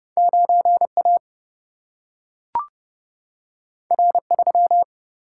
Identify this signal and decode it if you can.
Morse code, heard as a keyed tone, about 30 wpm, 700 Hz. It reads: 9A R3
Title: Single-Letter Single-Number 30wpm